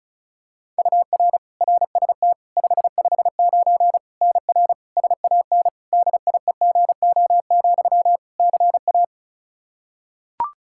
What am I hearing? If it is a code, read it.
Morse code, heard as a keyed tone, about 36 wpm, 700 Hz. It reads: UR RST 559 NR SAN DIEGO, CA
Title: Practice QSO - Exchange 1 - Beginning - 35wpm